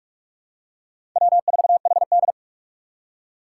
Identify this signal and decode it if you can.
Morse code, heard as a keyed tone, about 44 wpm, 700 Hz. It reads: W4HD
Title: Four Challenging Letter-Number Combinations 45wpm